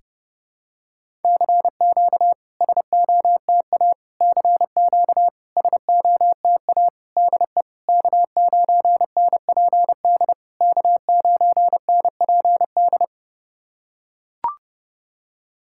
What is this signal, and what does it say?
Morse code, heard as a keyed tone, about 30 wpm, 690 Hz. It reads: CQ SOTA CQ SOTA DE K9NPD K9NPD
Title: SOTA QSO Practice 30wpm